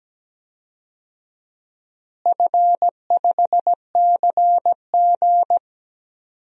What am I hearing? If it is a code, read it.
Morse code, heard as a keyed tone, about 17 wpm, 700 Hz. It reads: F5CG
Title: DXCC-100 Call Signs - 4 Characters - Repeated 3x - 17wpm